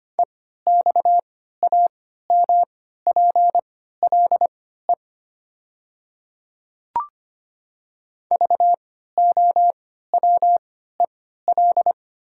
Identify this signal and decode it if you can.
Morse code, heard as a keyed tone, about 25 wpm, spaced as for 14 wpm, 700 Hz. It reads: EXAMPLE VOWEL
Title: 3x Character Spacing - Top 500 Words - 25wpm